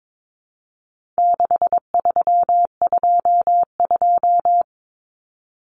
Morse code, heard as a keyed tone, about 22 wpm, 700 Hz. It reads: 6322
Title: Numbers - 4-Digit 22wpm